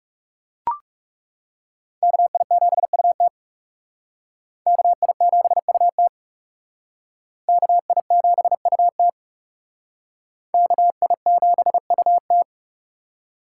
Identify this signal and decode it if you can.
Morse code, heard as a keyed tone, about 37 wpm, 700 Hz. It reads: KI7UT KI7UT KI7UT KI7UT
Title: Speed Racing - USA Call Signs 30wpm